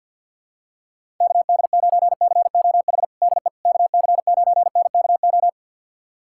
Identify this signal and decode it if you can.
Morse code, heard as a keyed tone, about 49 wpm, 700 Hz. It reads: KD9COH DE KC9NKO
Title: USA Callsign DE Callsign - 6 Characters - 50wpm